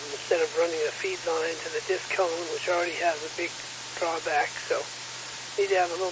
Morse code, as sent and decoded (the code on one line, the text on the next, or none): none